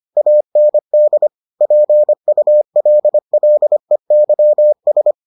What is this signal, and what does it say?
Morse code, heard as a keyed tone, about 25 wpm, 600 Hz. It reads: AND PULLEYS